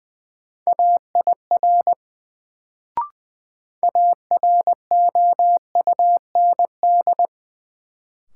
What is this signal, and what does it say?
Morse code, heard as a keyed tone, about 20 wpm, 700 Hz. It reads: AIR AROUND
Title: Sets of 2 Words - New Word Drill - 20wpm